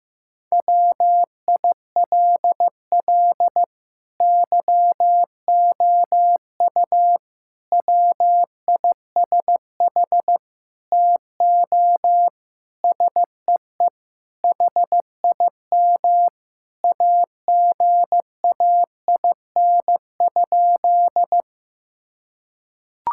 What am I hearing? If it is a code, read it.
Morse code, heard as a keyed tone, about 16 wpm, 700 Hz. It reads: WILL YOU WISH TO SEE HIM AGAIN?
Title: Sentences from Top 1600 Words 15wpm